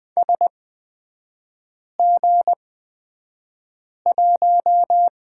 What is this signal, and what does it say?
Morse code, heard as a keyed tone, about 20 wpm, spaced as for 5 wpm, 700 Hz. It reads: SG1
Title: 243 Lesson 39e - Letter z - Three Characters 20wpm